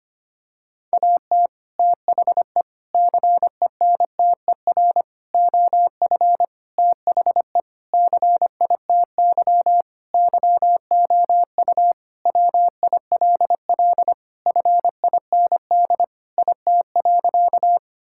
Morse code, heard as a keyed tone, about 25 wpm, 700 Hz. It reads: AT THE CENTER OF THE CITY YOU WILL FIND IT.
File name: practice / 25wpm / Sentences with 4-Word Phrases - 25wpm